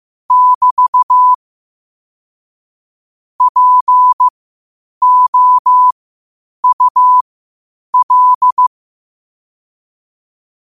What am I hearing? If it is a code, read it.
Morse code, heard as a keyed tone, about 15 wpm, spaced as for 8 wpm, 1000 Hz. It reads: <BT> POUL